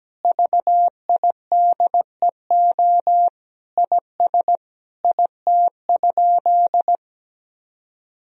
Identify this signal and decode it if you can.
Morse code, heard as a keyed tone, about 17 wpm, 700 Hz. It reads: VIDEO IS IT?
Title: Sentences from Top 1000 Words - Review Part 4 - 17wpm